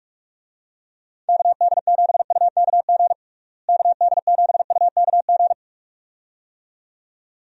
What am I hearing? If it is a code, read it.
Morse code, heard as a keyed tone, about 41 wpm, 700 Hz. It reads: KD7UKG KD7UKG
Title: USA Call Signs - 6 Characters - Repeated 2x - 45wpm